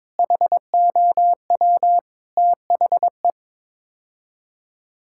Morse code, heard as a keyed tone, about 21 wpm, 700 Hz. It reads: HOW THE
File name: practice / 22wpm / Sets of 8 Words - 22wpm